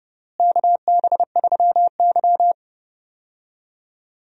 Morse code, heard as a keyed tone, about 30 wpm, 700 Hz. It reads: KB3Y
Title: USA Call Signs - 4 Characters - Repeated 3x - 30wpm